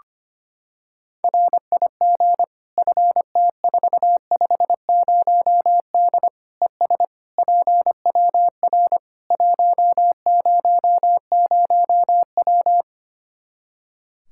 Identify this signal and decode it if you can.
Morse code, heard as a keyed tone, about 25 wpm, 700 Hz. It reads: RIG FT450D ES PWR 100W
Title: Practice QSO - Exchange 2 - Info - 25wpm